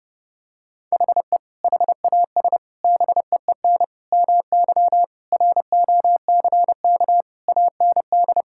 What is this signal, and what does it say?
Morse code, heard as a keyed tone, about 30 wpm, 700 Hz. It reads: HE HAS BEEN MY ROCK AND
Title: Sentences from Top 1000 Words - Review Part 5 - 30wpm